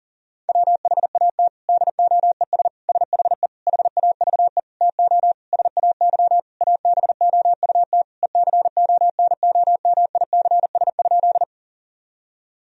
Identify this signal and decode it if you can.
Morse code, heard as a keyed tone, about 40 wpm, 700 Hz. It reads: WHAT DOES SHE HAVE TO SAY ABOUT ECONOMICS?
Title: Sentences from Top 1000 Words - Review Part 3 - 40wpm